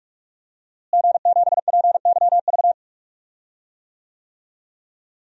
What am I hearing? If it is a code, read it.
Morse code, heard as a keyed tone, about 46 wpm, 700 Hz. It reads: G7PYV
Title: DXCC-100 Call Signs - 5 Characters - Repeated 2x - 45wpm